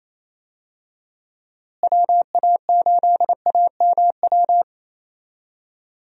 Morse code, heard as a keyed tone, about 27 wpm, 700 Hz. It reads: WA8AMW
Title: USA Call Signs - 6 Characters - Repeated 3x - 28wpm